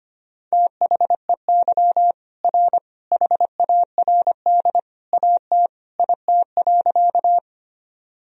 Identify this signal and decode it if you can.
Morse code, heard as a keyed tone, about 25 wpm, 700 Hz. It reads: THEY R HARD AT IT.
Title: Sentences from Top 300 Words and Abbreviations 25wpm